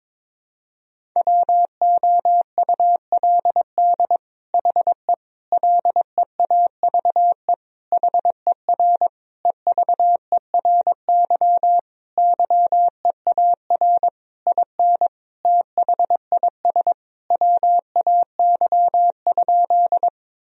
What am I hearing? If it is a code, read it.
Morse code, heard as a keyed tone, about 22 wpm, 700 Hz. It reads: WOULD HE LEAVE HER EVERY YEAR IN THIS WAY?
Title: Sentences from Top 1000 Words - Review Part 1 - 22wpm